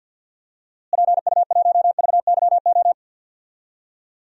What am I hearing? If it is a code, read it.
Morse code, heard as a keyed tone, about 48 wpm, 700 Hz. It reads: PU1VYO
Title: DXCC-100 Call Signs - 6 Characters - Repeated 2x - 50wpm